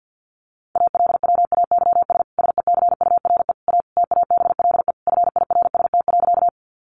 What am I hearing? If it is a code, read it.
Morse code, heard as a keyed tone, about 50 wpm, 700 Hz. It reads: ALWAYS SECURE A TABLE FIRST.
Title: Sentences from Top 1200 Words 50wpm